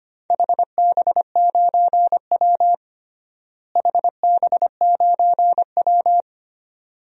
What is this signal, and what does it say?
Morse code, heard as a keyed tone, about 25 wpm, 700 Hz. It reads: HB9W HB9W
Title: DXCC-100 Call Signs - 4 Characters - Repeated 2x - 25wpm